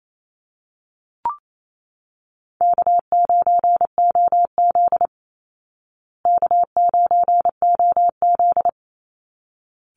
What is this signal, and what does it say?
Morse code, heard as a keyed tone, about 28 wpm, 700 Hz. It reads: K9OZ K9OZ
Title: Easy Call Signs Repeated 2x - 28wpm